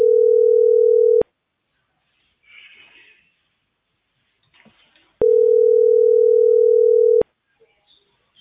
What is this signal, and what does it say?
Ringback tone (North America), 440 + 480 Hz.